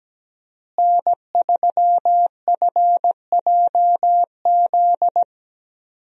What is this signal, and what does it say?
Morse code, heard as a keyed tone, about 17 wpm, 700 Hz. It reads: N3FJZ
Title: USA Call Signs - 5 Characters - Repeated 2x - 17wpm